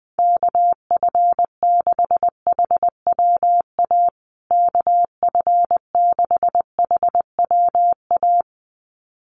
Morse code, heard as a keyed tone, about 20 wpm, 700 Hz. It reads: KF6HWA KF6HWA
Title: USA Call Signs Repeated 2x 20wpm